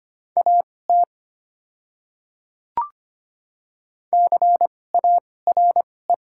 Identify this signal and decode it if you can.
Morse code, heard as a keyed tone, about 25 wpm, spaced as for 18 wpm, 700 Hz. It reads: AT CARE
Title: 2x Character Spacing - Top 300 Words - 25wpm